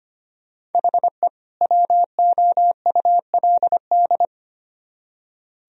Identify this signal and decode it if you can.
Morse code, heard as a keyed tone, about 32 wpm, 700 Hz. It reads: HE WOULD